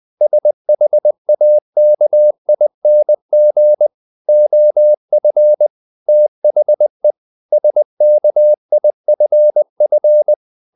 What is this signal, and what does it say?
Morse code, heard as a keyed tone, about 20 wpm, 600 Hz. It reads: SHAKING OF THE SKIFF
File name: the old man and the sea part 2 20wpm